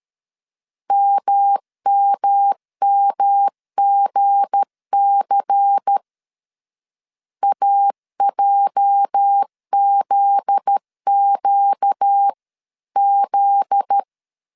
Morse code, heard as a keyed tone, about 13 wpm, 800 Hz. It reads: MMMGC AJZQZ
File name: CW / DET / 1 9 1500C S B 12wpm 3819